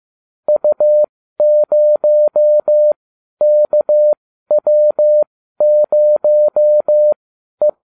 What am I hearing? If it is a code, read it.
Morse code, heard as a keyed tone, about 15 wpm, spaced as for 12 wpm, 600 Hz. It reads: U0KW0E